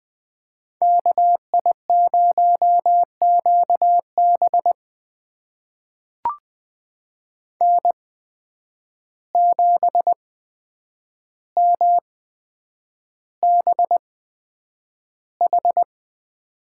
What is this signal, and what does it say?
Morse code, heard as a keyed tone, about 20 wpm, spaced as for 9 wpm, 700 Hz. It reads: KI0QB N7MBH
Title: USA Call Signs - 5 Characters - 8x Character Spacing - 20wpm